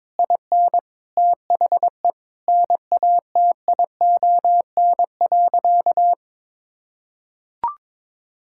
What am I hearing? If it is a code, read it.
Morse code, heard as a keyed tone, about 22 wpm, 700 Hz. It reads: IN THE NATION.
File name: practice / 22wpm / Sentences with 8-Word Phrases - 22wpm